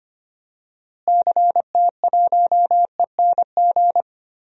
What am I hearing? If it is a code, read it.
Morse code, heard as a keyed tone, about 25 wpm, 700 Hz. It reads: CT1ENG
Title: DXCC-100 Call Signs - 6 Characters - Repeated 2x - 25wpm